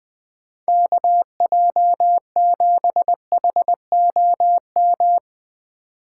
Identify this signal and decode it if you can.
Morse code, heard as a keyed tone, about 20 wpm, 700 Hz. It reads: KJ7HOM
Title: USA Call Signs - 6 Characters - 2x Character Spacing - 20wpm